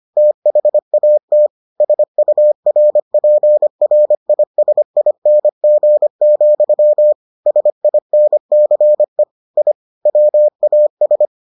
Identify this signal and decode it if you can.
Morse code, heard as a keyed tone, about 25 wpm, 600 Hz. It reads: THAT SURPRISING, SINCE I WAS